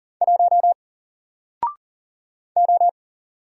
Morse code, heard as a keyed tone, about 40 wpm, 700 Hz. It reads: 1 O
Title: Single Letter-Number Alt 40wpm